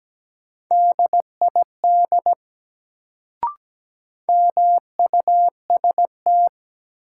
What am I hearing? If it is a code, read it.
Morse code, heard as a keyed tone, about 17 wpm, 700 Hz. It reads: DID MUST
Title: Sets of 2 Words - New Word Drill - 17wpm